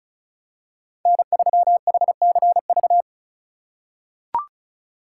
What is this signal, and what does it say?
Morse code, heard as a keyed tone, about 34 wpm, 700 Hz. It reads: N3HCV
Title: USA Call Signs - 5 Characters - 2x Character Spacing - 35wpm